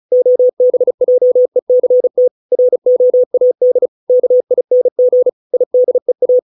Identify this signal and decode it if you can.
Morse code, heard as a keyed tone, about 35 wpm, 500 Hz. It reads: OBJECT ROAD KING IDEA